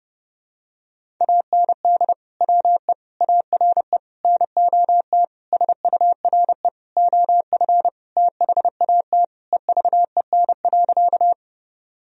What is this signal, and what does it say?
Morse code, heard as a keyed tone, about 30 wpm, 700 Hz. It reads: AND WE ARE NOT SURE OF THAT EVEN.
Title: Sentences from Top 1000 Words - Review Part 1 - 30wpm